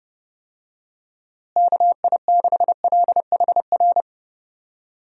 Morse code, heard as a keyed tone, about 30 wpm, 700 Hz. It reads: KI6LHR
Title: USA Call Signs - 6 Characters - Repeated 3x - 30wpm